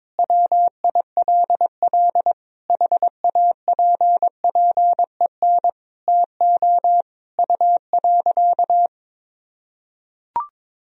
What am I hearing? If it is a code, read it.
Morse code, heard as a keyed tone, about 22 wpm, 700 Hz. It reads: WILL HAPPEN TO U.
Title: Sentences from Top 500 Words and Abbreviations 22wpm